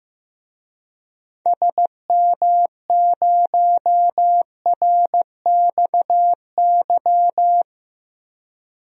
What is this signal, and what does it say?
Morse code, heard as a keyed tone, about 15 wpm, 700 Hz. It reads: SM0RXY
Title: DXCC-100 Call Signs - 6 Characters - Repeated 3x - 15wpm